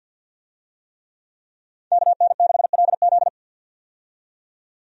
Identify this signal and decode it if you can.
Morse code, heard as a keyed tone, about 52 wpm, 700 Hz. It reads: KN6LZ
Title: USA Call Signs - 5 Characters - 50wpm